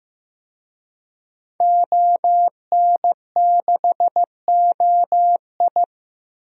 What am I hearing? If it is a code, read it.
Morse code, heard as a keyed tone, about 15 wpm, 700 Hz. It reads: ON6OI
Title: DXCC-100 Call Signs - 5 Characters - Repeated 2x - 15wpm